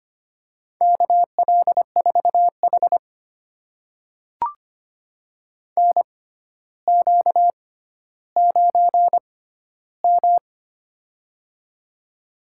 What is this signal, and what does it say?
Morse code, heard as a keyed tone, about 25 wpm, spaced as for 13 wpm, 700 Hz. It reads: KL4H NQ9M
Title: USA Call Signs - 4 Characters - 6x Character Spacing - 25wpm